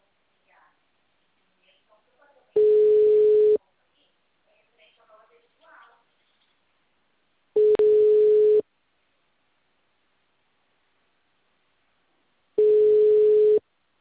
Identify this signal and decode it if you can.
Ringback tone (Europe, 425 Hz), with speech.